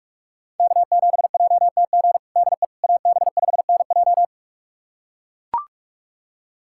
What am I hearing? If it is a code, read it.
Morse code, heard as a keyed tone, about 42 wpm, 700 Hz. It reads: K7JTG DE AB5NJ
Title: USA Callsign DE Callsign - 5 Characters - 45wpm